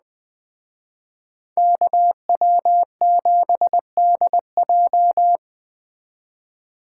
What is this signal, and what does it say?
Morse code, heard as a keyed tone, about 20 wpm, 700 Hz. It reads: KW7DJ
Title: Speed Racing - USA 2x2 Call Signs 17wpm